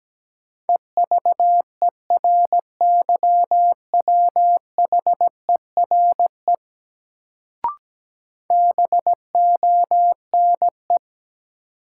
Morse code, heard as a keyed tone, about 17 wpm, 700 Hz. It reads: EVERYWHERE BONE
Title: Sentences from Top 1600 Words - New Word Drill Alt - 17wpm